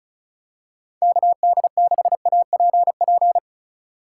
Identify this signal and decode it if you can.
Morse code, heard as a keyed tone, about 36 wpm, 700 Hz. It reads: KD6APP
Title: USA Call Signs - 6 Characters - 3x Character Spacing - 35wpm